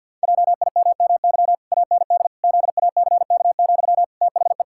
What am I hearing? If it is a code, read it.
Morse code, heard as a keyed tone, about 50 wpm, 700 Hz. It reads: JIMMY AND ZACK, THE
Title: Pangram Challenge v2 - 50wpm